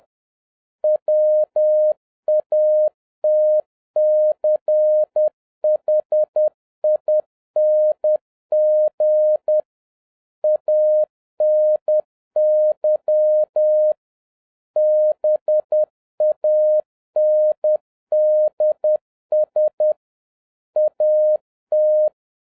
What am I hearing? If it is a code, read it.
Morse code, heard as a keyed tone, about 10 wpm, 610 Hz. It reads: WATCHING ANY BANDS AT
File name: 2011-07-mp3 / n56 BACK FROM A BIG ACCIDENT 10wpm